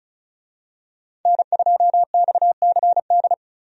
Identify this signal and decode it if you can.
Morse code, heard as a keyed tone, about 36 wpm, 700 Hz. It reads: N2XCD
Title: USA Call Signs - 5 Characters - 35wpm